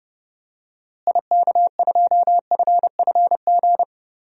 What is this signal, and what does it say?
Morse code, heard as a keyed tone, about 30 wpm, 700 Hz. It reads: IK2FFG
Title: DXCC-100 Call Signs - 6 Characters - Repeated 2x - 30wpm